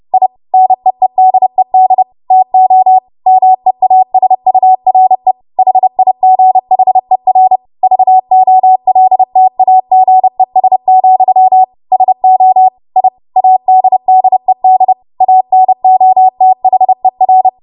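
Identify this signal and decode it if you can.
Morse code, heard as a keyed tone, about 30 wpm, 750 Hz. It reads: I NEEDED TO MEASURE HIGHER VOLTAGES, SO I ADDED ANOTHER
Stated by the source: W1AW Web Code Practice Files, 30 WPM run from ARRL; text from April 2023 QST.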